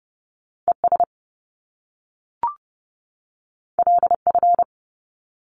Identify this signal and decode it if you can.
Morse code, heard as a keyed tone, about 30 wpm, 700 Hz. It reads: ES LF
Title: Top Two Letter Combinations in Words 30wpm